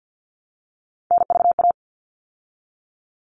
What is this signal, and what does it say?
Morse code, heard as a keyed tone, about 50 wpm, 700 Hz. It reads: NVA
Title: All Cut Numbers - 3-Digit 50wpm